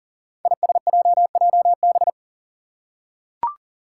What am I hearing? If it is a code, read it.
Morse code, heard as a keyed tone, about 40 wpm, 700 Hz. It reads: ISJJB
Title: Five Challenging Letter-Number Combinations 40wpm